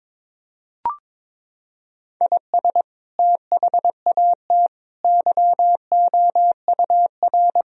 Morse code, heard as a keyed tone, about 22 wpm, 700 Hz. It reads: IS THAT YOUR
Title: Sentences from Top 1000 Words - Review Part 2 - 22wpm